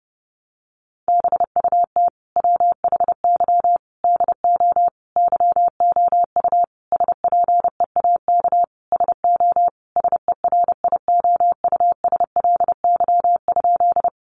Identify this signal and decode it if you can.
Morse code, heard as a keyed tone, about 30 wpm, 700 Hz. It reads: BUT WHY DO YOU SPEAK SO SERIOUSLY?
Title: Sentences from Top 1200 Words 30wpm